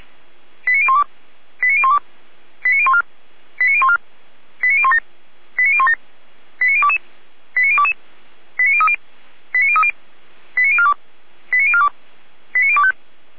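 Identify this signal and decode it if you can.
Five-tone selective call (ZVEI), heard as quick runs of five tones, tones of about 80 ms. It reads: 89012 89012 89014 89014 89017 89017 89022 89022 89030 89030 89042 89042 89025
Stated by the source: ZVEI five-tone selective call received on SDR.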